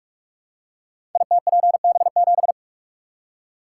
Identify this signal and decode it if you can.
Morse code, heard as a keyed tone, about 44 wpm, 700 Hz. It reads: ITPB7
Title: Code Groups - 5 Letter:Numbers - 45wpm